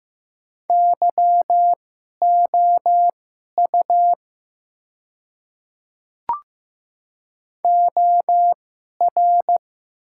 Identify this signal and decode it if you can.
Morse code, heard as a keyed tone, about 15 wpm, spaced as for 11 wpm, 700 Hz. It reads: YOU OR
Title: 2x Character Spacing - Top 100 Words - 15wpm